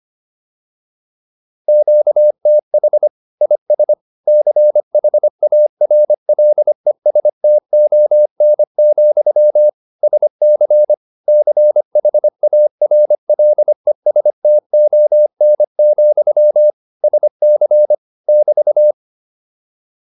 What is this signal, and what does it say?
Morse code, heard as a keyed tone, about 25 wpm, 600 Hz. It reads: QTH IS CHARLESTON, SC CHARLESTON, SC <BT>